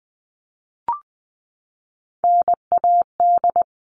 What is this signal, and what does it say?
Morse code, heard as a keyed tone, about 20 wpm, 700 Hz. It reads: NAD
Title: All Cut Numbers - 3-Digit 20wpm